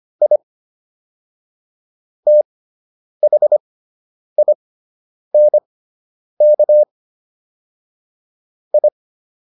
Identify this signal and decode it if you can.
Morse code, heard as a keyed tone, about 25 wpm, spaced as for 9 wpm, 600 Hz. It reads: I THINK I